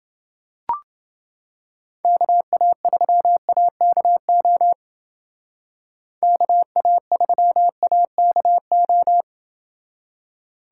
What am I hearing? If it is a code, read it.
Morse code, heard as a keyed tone, about 30 wpm, 700 Hz. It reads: KA3AKO KA3AKO